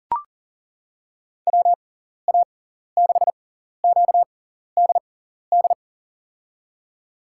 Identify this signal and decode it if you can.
Morse code, heard as a keyed tone, about 37 wpm, spaced as for 14 wpm, 700 Hz. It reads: WA6QDD
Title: USA Call Signs - 6 Characters - 6x Character Spacing - 40wpm